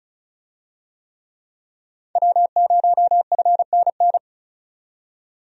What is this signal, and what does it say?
Morse code, heard as a keyed tone, about 36 wpm, 700 Hz. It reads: W0FNN
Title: USA Call Signs - 5 Characters - 35wpm